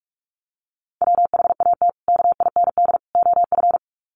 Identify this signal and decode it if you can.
Morse code, heard as a keyed tone, about 44 wpm, 700 Hz. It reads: WHAT KIND OF
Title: Sets of 3 Words 45wpm